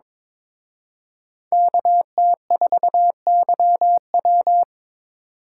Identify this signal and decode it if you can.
Morse code, heard as a keyed tone, about 26 wpm, 700 Hz. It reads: KT4YW
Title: Speed Racing - USA 2x2 Call Signs 22wpm